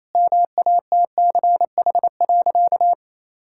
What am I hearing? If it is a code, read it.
Morse code, heard as a keyed tone, about 28 wpm, 700 Hz. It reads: MATCH.
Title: Sentences from Top 1600 Words 28wpm